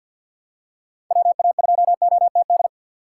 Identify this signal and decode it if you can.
Morse code, heard as a keyed tone, about 53 wpm, 700 Hz. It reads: WA2OTD
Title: USA Call Signs - 6 Characters - 6x Character Spacing - 50wpm